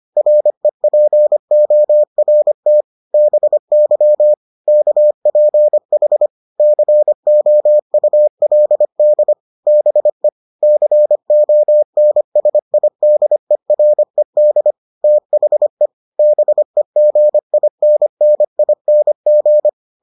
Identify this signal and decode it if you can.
Morse code, heard as a keyed tone, about 25 wpm, 600 Hz. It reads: REPORT BY KPH COULD BE CONSIDERED THE BEGINNING